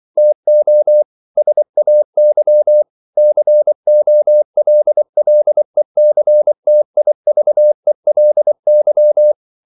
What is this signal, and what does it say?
Morse code, heard as a keyed tone, about 24 wpm, 600 Hz. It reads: TO SAY COLLECTIVELY
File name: CW / Practice / EnglishQuotations0424